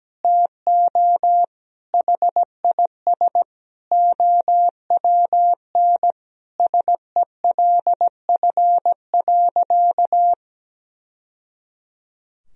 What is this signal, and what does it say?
Morse code, heard as a keyed tone, about 17 wpm, 700 Hz. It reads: TO HIS OWN SELF.
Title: Sentences from Top 1000 Words - Review Part 1 - 17wpm